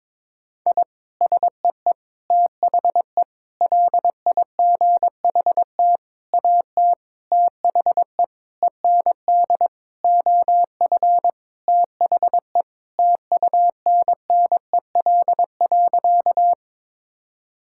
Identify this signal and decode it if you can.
Morse code, heard as a keyed tone, about 22 wpm, 700 Hz. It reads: I SEE THE LIGHT AT THE END OF THE TUNNEL.
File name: practice / 22wpm / Sentences with 8-Word Phrases - 22wpm